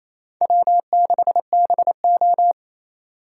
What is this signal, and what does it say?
Morse code, heard as a keyed tone, about 35 wpm, 700 Hz. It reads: W6BO
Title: Speed Racing - USA 1x2 Call Signs 28wpm